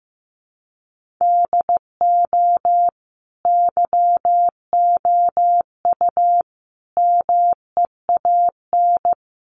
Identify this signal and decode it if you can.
Morse code, heard as a keyed tone, about 15 wpm, 700 Hz. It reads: DO YOU MEAN
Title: Sets of 5 Words Encore2 15wpm